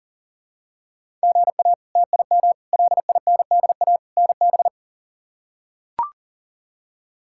Morse code, heard as a keyed tone, about 40 wpm, 700 Hz. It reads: GA TIM LINDA NB
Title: K1USN SST QSO Practice - 40wpm